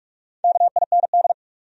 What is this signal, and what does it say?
Morse code, heard as a keyed tone, about 59 wpm, 700 Hz. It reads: KIND